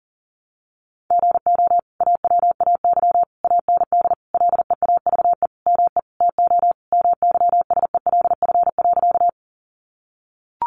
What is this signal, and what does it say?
Morse code, heard as a keyed tone, about 37 wpm, 700 Hz. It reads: GO AWAY AND LEAVE ME TO MYSELF.
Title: Sentences from Top 1300 Words 40wpm